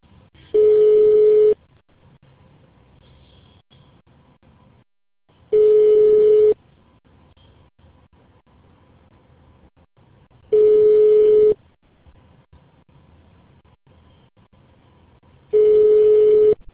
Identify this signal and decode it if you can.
Ringback tone (Europe, 425 Hz).